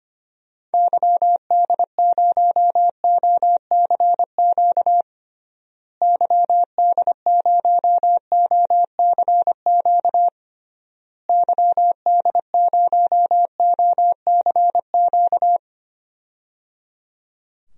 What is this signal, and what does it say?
Morse code, heard as a keyed tone, about 25 wpm, 700 Hz. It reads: YD0OCQ YD0OCQ YD0OCQ
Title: DXCC-100 Call Signs - 6 Characters - Repeated 3x - 25wpm